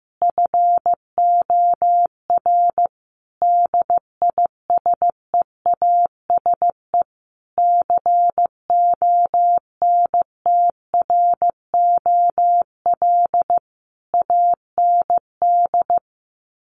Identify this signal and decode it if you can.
Morse code, heard as a keyed tone, about 15 wpm, 700 Hz. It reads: FOR DISEASE CONTROL AND
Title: Sets of 5 Words Encore2 15wpm